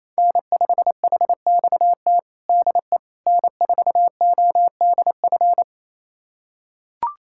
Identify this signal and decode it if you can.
Morse code, heard as a keyed tone, about 28 wpm, 700 Hz. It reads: N5HXT DE N4ODF
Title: USA Callsign DE Callsign - 5 Characters - 28wpm